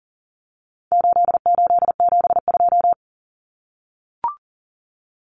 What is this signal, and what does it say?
Morse code, heard as a keyed tone, about 40 wpm, 700 Hz. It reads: 8872
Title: Numbers - 4-Digit 40wpm